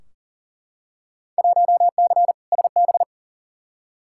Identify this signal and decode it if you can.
Morse code, heard as a keyed tone, about 39 wpm, 700 Hz. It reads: 1C SB
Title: Field Day Exchanges Encore - 40wpm